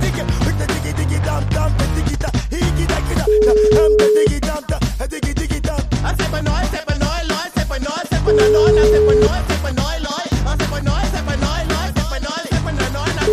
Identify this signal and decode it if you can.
Ringback tone (Europe, 425 Hz), with noise.